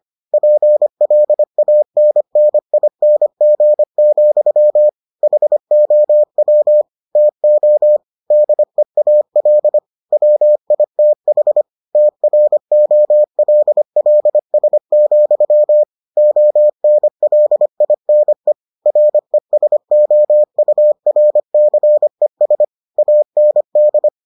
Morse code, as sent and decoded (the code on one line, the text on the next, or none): .--. .-.. .- -. -. .. -. --. --..-- / .... --- .-- / - --- / -.. .- .-.. / .-- .. - .... / - .-. --- .-.. .-.. ... --..-- / --- -. .-.. .. -. / .-. ... --- ..- .-. -.-. ... / .- -. -..
PLANNING, HOW TO DEAL WITH TROLLS, ONLINE RESOURCES AND